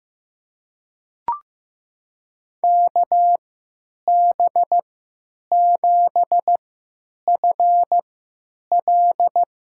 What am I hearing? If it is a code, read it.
Morse code, heard as a keyed tone, about 15 wpm, spaced as for 9 wpm, 700 Hz. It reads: KB7FL
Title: USA Call Signs - 3x Character Spacing - 15wpm